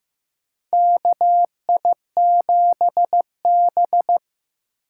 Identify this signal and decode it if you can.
Morse code, heard as a keyed tone, about 15 wpm, 700 Hz. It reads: KI7B
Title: USA Call Signs - 4 Characters - 8x Character Spacing - 15wpm